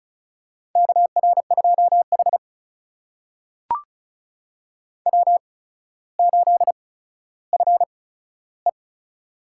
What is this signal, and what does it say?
Morse code, heard as a keyed tone, about 36 wpm, spaced as for 15 wpm, 700 Hz. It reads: KR2H W8FE
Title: USA Call Signs - 4 Characters - 8x Character Spacing - 35wpm